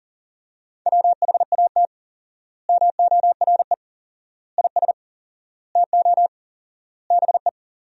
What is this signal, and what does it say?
Morse code, heard as a keyed tone, about 37 wpm, 700 Hz. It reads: WHAT MORE IS TO BE
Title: 4x Word-Spacing - Sentences from Top 500 Words - 40wpm